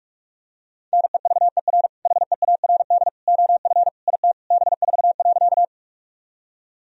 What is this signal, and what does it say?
Morse code, heard as a keyed tone, about 46 wpm, 700 Hz. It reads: NEVER HEARD OF IT B4.
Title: Sentences from Top 400 Words and Abbreviations 45wpm